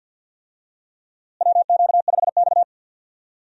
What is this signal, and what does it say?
Morse code, heard as a keyed tone, about 50 wpm, 700 Hz. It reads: WQ5X
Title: USA Call Signs - 4 Characters - Repeated 2x - 50wpm